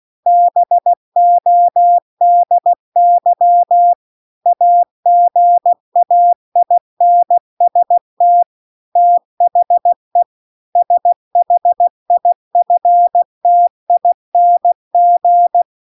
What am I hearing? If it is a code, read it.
Morse code, heard as a keyed tone, about 16 wpm, 700 Hz. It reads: BODY AGAINST THE SHIFTING
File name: CW-The-Big-Twoheaded-River-2-16WPM